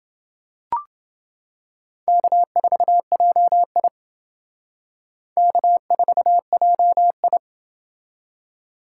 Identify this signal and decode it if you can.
Morse code, heard as a keyed tone, about 28 wpm, 700 Hz. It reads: K4JI K4JI